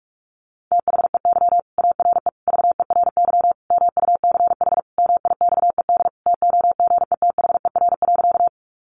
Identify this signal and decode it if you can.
Morse code, heard as a keyed tone, about 44 wpm, 700 Hz. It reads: THEY ARE VERY MUCH MIXED TOGETHER.
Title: Sentences from Top 1500 Words 45wpm